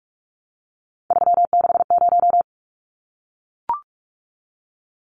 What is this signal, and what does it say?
Morse code, heard as a keyed tone, about 44 wpm, 700 Hz. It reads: 360
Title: Numbers - 3-Digit 45wpm